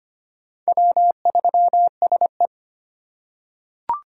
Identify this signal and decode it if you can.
Morse code, heard as a keyed tone, about 25 wpm, 700 Hz. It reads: W3SE
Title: USA Call Signs - 4x Character Spacing - 25wpm